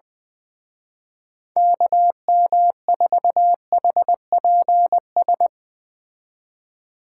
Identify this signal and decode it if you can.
Morse code, heard as a keyed tone, about 20 wpm, 700 Hz. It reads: KM4HPS
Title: Speed Racing - USA 2x3 Call Signs 17wpm